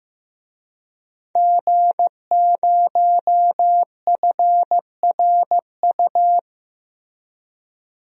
Morse code, heard as a keyed tone, about 15 wpm, 700 Hz. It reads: G0FRU
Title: DXCC-100 Call Signs - 5 Characters - 15wpm